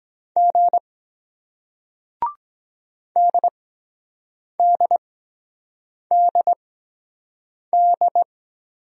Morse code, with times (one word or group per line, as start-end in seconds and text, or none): 0.4-0.8 G
3.2-8.2 DDDD